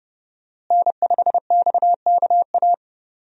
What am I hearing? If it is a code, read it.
Morse code, heard as a keyed tone, about 30 wpm, 700 Hz. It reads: N5XKA
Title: USA Call Signs - 5 Characters - Repeated 2x - 30wpm